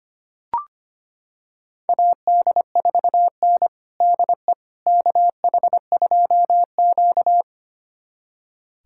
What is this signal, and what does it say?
Morse code, heard as a keyed tone, about 25 wpm, 700 Hz. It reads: AD4N DE KH2Q
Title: USA Callsign DE Callsign - 4 Characters - 25wpm